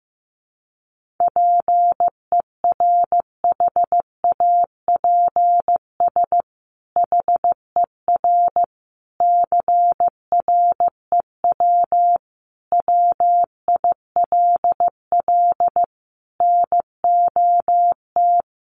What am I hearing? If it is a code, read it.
Morse code, heard as a keyed tone, about 15 wpm, 700 Hz. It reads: PERHAPS HER CREW WILL NOT
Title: Sentences from Top 1500 Words 15wpm